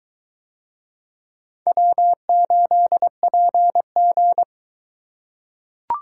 Morse code, heard as a keyed tone, about 22 wpm, 700 Hz. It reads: W8PG